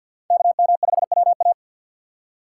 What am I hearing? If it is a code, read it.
Morse code, heard as a keyed tone, about 51 wpm, 700 Hz. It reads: KM5WA
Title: USA Call Signs - 5 Characters - 6x Character Spacing - 50wpm